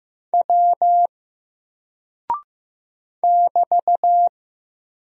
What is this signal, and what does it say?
Morse code, heard as a keyed tone, about 15 wpm, 700 Hz. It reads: W <BT>
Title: Single Character (Letter:Number:Punctuation:Prosign) Alt 15wpm